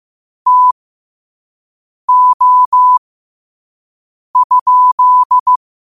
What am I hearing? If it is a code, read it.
Morse code, heard as a keyed tone, about 15 wpm, spaced as for 5 wpm, 1000 Hz. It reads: TO?
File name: qotdfiles / 5wpm